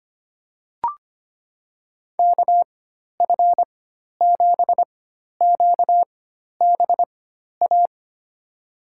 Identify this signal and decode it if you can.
Morse code, heard as a keyed tone, about 25 wpm, spaced as for 12 wpm, 700 Hz. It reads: KF7QBA
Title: USA Call Signs - 6 Characters - 4x Character Spacing - 25wpm